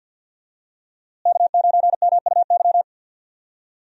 Morse code, heard as a keyed tone, about 52 wpm, 700 Hz. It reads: K9MUY
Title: USA Call Signs - 5 Characters - Repeated 3x - 50wpm